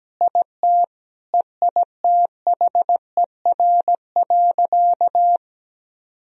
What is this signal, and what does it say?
Morse code, heard as a keyed tone, about 17 wpm, 700 Hz. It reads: IT EITHER.
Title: Sentences from Top 1000 Words - Review Part 4 - 17wpm